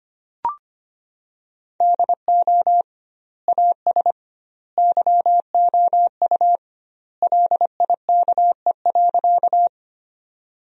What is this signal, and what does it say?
Morse code, heard as a keyed tone, about 25 wpm, 700 Hz. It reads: DO AS YOU LIKE.
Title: 2x Word-Spacing - Sentences from Top 100 Words - 25wpm